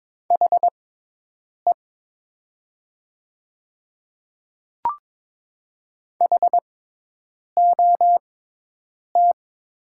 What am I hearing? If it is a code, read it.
Morse code, heard as a keyed tone, about 22 wpm, spaced as for 8 wpm, 700 Hz. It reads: HE HOT
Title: 6x Character Spacing - Top 100 Words - 22wpm